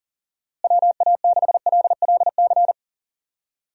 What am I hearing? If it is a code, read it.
Morse code, heard as a keyed tone, about 40 wpm, 700 Hz. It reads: WA6LLC
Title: USA Call Signs - 6 Characters - Repeated 3x - 40wpm